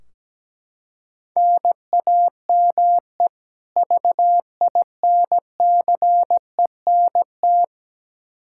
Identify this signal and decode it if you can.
Morse code, heard as a keyed tone, about 17 wpm, 700 Hz. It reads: NAME VINCENT
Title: Practice QSO - Exchange 1 - Beginning - 17wpm